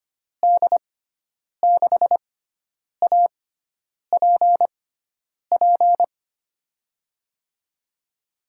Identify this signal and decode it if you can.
Morse code, heard as a keyed tone, about 25 wpm, spaced as for 9 wpm, 700 Hz. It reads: D6APP
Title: USA Call Signs - 6 Characters - 6x Character Spacing - 25wpm